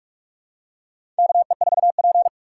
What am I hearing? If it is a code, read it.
Morse code, heard as a keyed tone, about 46 wpm, 700 Hz. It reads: KE4P
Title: USA Call Signs - 4 Characters - 2x Character Spacing - 45wpm